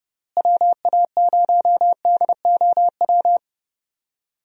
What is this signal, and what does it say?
Morse code, heard as a keyed tone, about 30 wpm, 700 Hz. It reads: WA0DOW
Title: USA Call Signs - 6 Characters - Repeated 2x - 30wpm